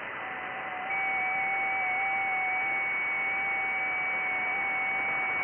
FT8.
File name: IC-7100